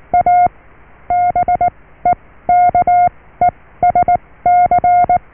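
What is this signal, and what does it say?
Morse code, heard as a keyed tone, about 19 wpm, spaced as for 14 wpm, 710 Hz. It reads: A BEKESC
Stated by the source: Morse QSO on 14 MHz with a German station, received at RST 579.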